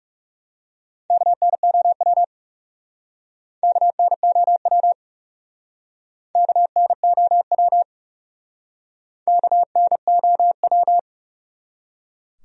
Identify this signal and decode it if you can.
Morse code, heard as a keyed tone, about 40 wpm, 700 Hz. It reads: KNOW KNOW KNOW KNOW